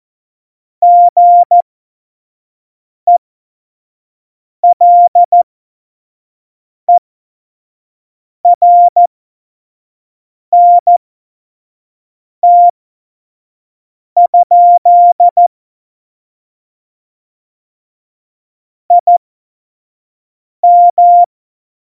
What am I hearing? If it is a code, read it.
Morse code, heard as a keyed tone, about 14 wpm, spaced as for 5 wpm, 700 Hz. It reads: GELERNT? IM